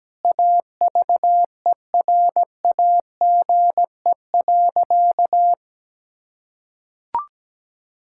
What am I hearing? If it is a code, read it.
Morse code, heard as a keyed tone, about 17 wpm, 700 Hz. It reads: AVERAGE.
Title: Sentences from Top 1000 Words - Review Part 6 - 17wpm